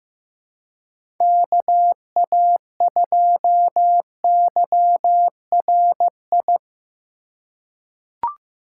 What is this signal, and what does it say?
Morse code, heard as a keyed tone, about 15 wpm, 700 Hz. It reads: KA2YRI
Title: USA Call Signs - 6 Characters - 3x Character Spacing - 15wpm